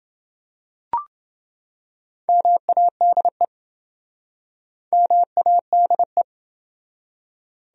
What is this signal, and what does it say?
Morse code, heard as a keyed tone, about 26 wpm, 700 Hz. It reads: MADE MADE